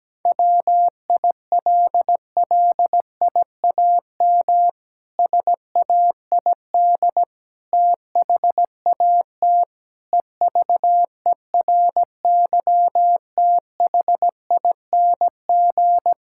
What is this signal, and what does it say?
Morse code, heard as a keyed tone, about 17 wpm, 700 Hz. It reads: WILLIAM SAID THAT EVERYTHING
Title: Pangram Challenge v2 - 17wpm